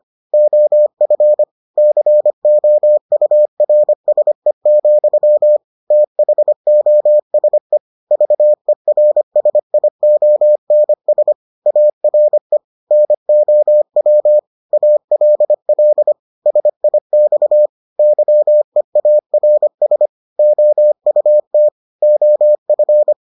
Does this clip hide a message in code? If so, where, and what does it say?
Yes: Morse code from 0.3 to 23.1 s, heard as a keyed tone, about 25 wpm, 600 Hz. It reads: OF COURSE, THOSE VERSIONS ARE NOW ALL SIX YEARS OUT OF